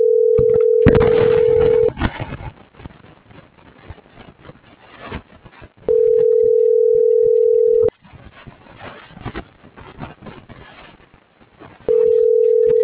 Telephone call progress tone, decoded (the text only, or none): ringback tone